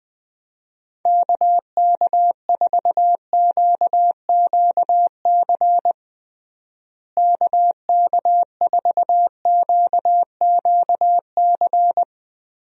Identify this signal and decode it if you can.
Morse code, heard as a keyed tone, about 20 wpm, 700 Hz. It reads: KK4QQC KK4QQC
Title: USA Call Signs - 6 Characters - Repeated 3x - 20wpm